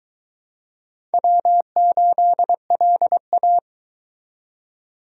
Morse code, heard as a keyed tone, about 22 wpm, 700 Hz. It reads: W8LA